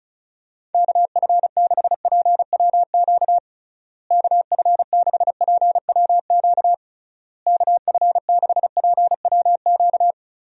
Morse code, heard as a keyed tone, about 36 wpm, 700 Hz. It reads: KF6PWQ KF6PWQ KF6PWQ
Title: USA Call Signs - 6 Characters - Repeated 3x - 35wpm